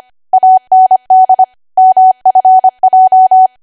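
Morse code, heard as a keyed tone, about 25 wpm, 750 Hz. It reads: AND MFJ
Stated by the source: ARRL 25 WPM code practice archive.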